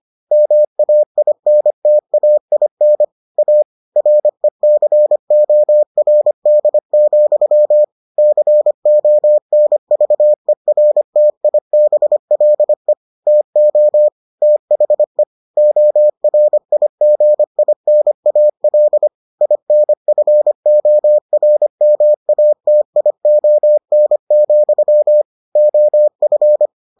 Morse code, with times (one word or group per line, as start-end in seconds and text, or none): 0.3-3.0 MAINTAIN
3.4-3.6 A
4.0-7.8 RECORD,
8.2-12.9 CONVERTIBLE
13.3-14.1 TO
14.4-15.2 THE
15.6-19.1 ORIGINAL
19.4-25.2 INFORMATION,
25.6-26.7 OF